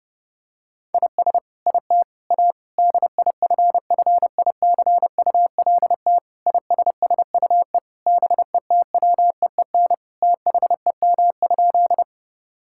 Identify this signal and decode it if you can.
Morse code, heard as a keyed tone, about 32 wpm, 700 Hz. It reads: IS IT A DIFFICULT ISSUE BETWEEN THEM?
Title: Sentences from Top 1000 Words - Review Part 6 - 30wpm